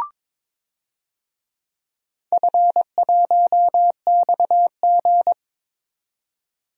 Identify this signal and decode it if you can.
Morse code, heard as a keyed tone, about 22 wpm, 690 Hz. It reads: F1XG
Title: DXCC-100 Call Signs - 4 Characters - 22wpm